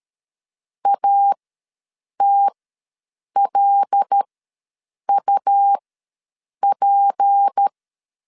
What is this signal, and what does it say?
Morse code, heard as a keyed tone, about 13 wpm, spaced as for 7 wpm, 800 Hz. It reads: ATLUP
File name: LEZIONI / 6 PKX0 / 6 P ws 12wpm 3117